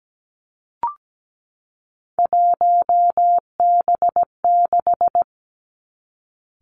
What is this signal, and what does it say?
Morse code, heard as a keyed tone, about 17 wpm, 700 Hz. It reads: 1B6
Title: Three Challenging Letter-Number Combinations 17wpm